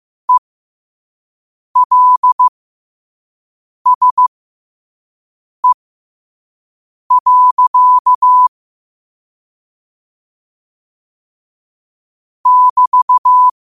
Morse code, heard as a keyed tone, about 15 wpm, spaced as for 5 wpm, 1000 Hz. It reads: ELSE. <BT>